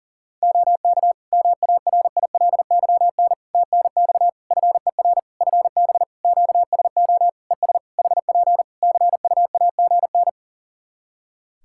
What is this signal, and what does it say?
Morse code, heard as a keyed tone, about 41 wpm, 700 Hz. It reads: OK MARILYN TNX FER FB QSO ES HP CUAGN
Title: Practice QSO - Exchange 4 - Ending - 40wpm